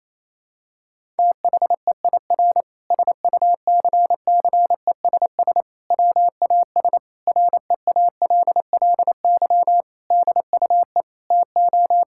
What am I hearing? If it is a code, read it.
Morse code, heard as a keyed tone, about 28 wpm, 700 Hz. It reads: THEIR SUCCESS WAS REALLY DUE TO
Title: Sentences from Top 1000 Words - Review Part 6 - 28wpm